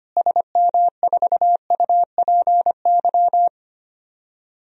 Morse code, heard as a keyed tone, about 25 wpm, 700 Hz. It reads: SM4UPY
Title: DXCC-100 Call Signs - 6 Characters - Repeated 2x - 25wpm